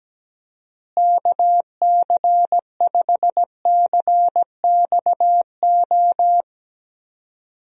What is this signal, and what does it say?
Morse code, heard as a keyed tone, about 17 wpm, 700 Hz. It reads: KC5CXO
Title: USA Call Signs - 6 Characters - Repeated 2x - 17wpm